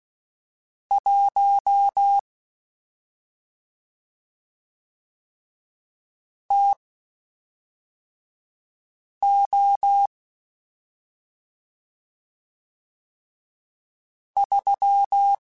Morse code, heard as a keyed tone, about 16 wpm, spaced as for 3 wpm, 800 Hz. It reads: 1 TO 3
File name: Spaced Characters / Spaced Characters-mp3s / Weather5 1 sp.@5